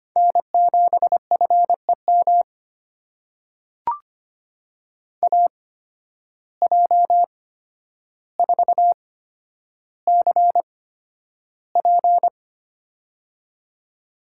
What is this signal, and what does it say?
Morse code, heard as a keyed tone, about 25 wpm, spaced as for 11 wpm, 700 Hz. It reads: N7FEM AJ4CP
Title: USA Call Signs - 5 Characters - 8x Character Spacing - 25wpm